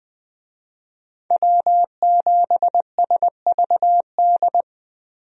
Morse code, heard as a keyed tone, about 20 wpm, 700 Hz. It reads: W7SVD
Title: USA Call Signs - 5 Characters - Repeated 2x - 20wpm